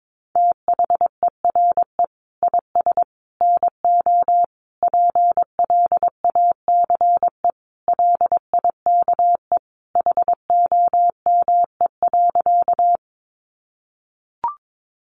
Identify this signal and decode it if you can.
Morse code, heard as a keyed tone, about 22 wpm, 700 Hz. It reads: THERE IS NO PLACE LIKE HOME.
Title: Sentences from Top 200 Words 22wpm